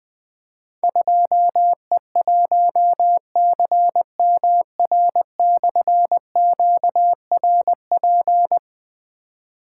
Morse code, heard as a keyed tone, about 20 wpm, 700 Hz. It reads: 2E1CMR/QRP
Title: Difficult Call Signs 20wpm